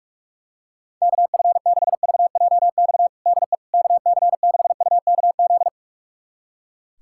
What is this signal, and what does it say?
Morse code, heard as a keyed tone, about 41 wpm, 700 Hz. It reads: KF6VJX DE KC6UKZ
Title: USA Callsign DE Callsign - 6 Characters - 45wpm